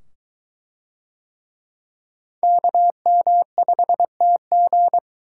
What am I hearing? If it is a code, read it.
Morse code, heard as a keyed tone, about 21 wpm, 700 Hz. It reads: KM5TG